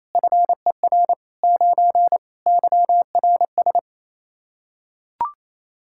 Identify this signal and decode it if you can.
Morse code, heard as a keyed tone, about 28 wpm, 700 Hz. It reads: FER 9 YRS
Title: K1USN SST QSO Practice 28wpm